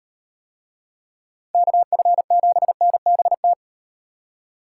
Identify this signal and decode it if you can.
Morse code, heard as a keyed tone, about 35 wpm, 700 Hz. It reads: KF7NBT